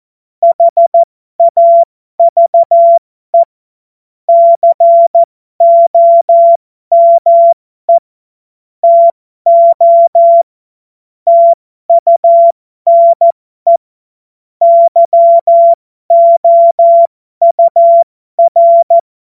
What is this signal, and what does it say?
Morse code, heard as a keyed tone, about 14 wpm, spaced as for 12 wpm, 680 Hz. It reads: HAVE COME TO TUNE YOUR
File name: Morsekurs / Witze